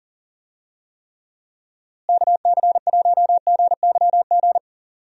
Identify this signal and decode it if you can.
Morse code, heard as a keyed tone, about 38 wpm, 700 Hz. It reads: KC1GYG
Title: USA Call Signs - 6 Characters - Repeated 2x - 40wpm